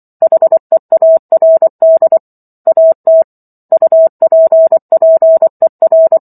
Morse code, heard as a keyed tone, about 24 wpm, 650 Hz. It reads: HEARD AT UPPER